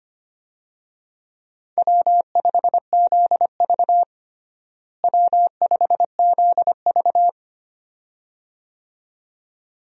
Morse code, heard as a keyed tone, about 25 wpm, 700 Hz. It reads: W5ZV W5ZV
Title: USA Call Signs - 4 Characters - Repeated 2x - 25wpm